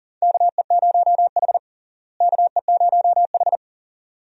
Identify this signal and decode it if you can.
Morse code, heard as a keyed tone, about 40 wpm, 700 Hz. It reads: KE0H KE0H
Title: USA Call Signs - 4 Characters - Repeated 2x - 40wpm